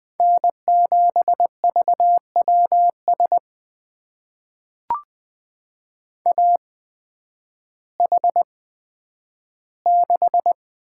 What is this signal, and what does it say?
Morse code, heard as a keyed tone, about 20 wpm, spaced as for 11 wpm, 700 Hz. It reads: N7VWS AH6
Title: USA Call Signs - 5 Characters - 8x Character Spacing - 20wpm